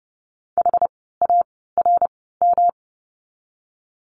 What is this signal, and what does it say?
Morse code, heard as a keyed tone, about 30 wpm, spaced as for 17 wpm, 700 Hz. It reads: HARM
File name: course / 30:10wpm / 108 Lesson 18f - Letter m - Words 30wpm